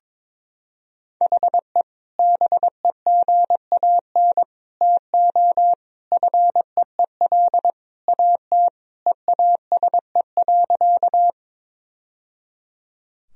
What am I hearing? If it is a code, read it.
Morse code, heard as a keyed tone, about 22 wpm, 700 Hz. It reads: HE BEGAN TO FEEL AT EASE.
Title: Sentences from Top 1000 Words - Review Part 1 - 22wpm